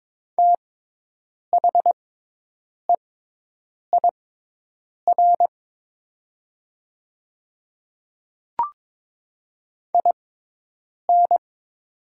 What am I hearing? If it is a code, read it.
Morse code, heard as a keyed tone, about 22 wpm, spaced as for 8 wpm, 700 Hz. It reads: THEIR IN
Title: 6x Character Spacing - Top 200 Words - 22wpm